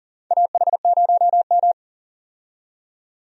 Morse code, heard as a keyed tone, about 40 wpm, 700 Hz. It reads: AH0M
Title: USA Call Signs - 4 Characters - 40wpm